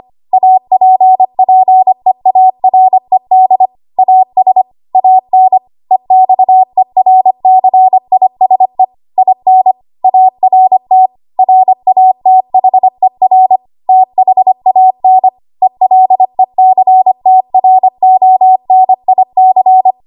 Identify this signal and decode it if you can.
Morse code, heard as a keyed tone, about 25 wpm, 750 Hz. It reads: APPEARED AS AN EXERCISE IN ART RATHER THAN ELECTRONIC
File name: Archive / 25 WPM / 25WPM